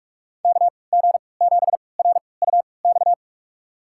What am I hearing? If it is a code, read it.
Morse code, heard as a keyed tone, about 47 wpm, spaced as for 26 wpm, 700 Hz. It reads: KG7RUX
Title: USA Call Signs - 6 Characters - 3x Character Spacing - 45wpm